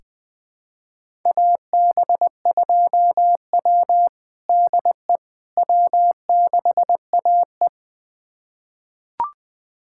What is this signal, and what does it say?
Morse code, heard as a keyed tone, about 20 wpm, 700 Hz. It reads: AB2W DE W6AE
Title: USA Callsign DE Callsign - 4 Characters - 20wpm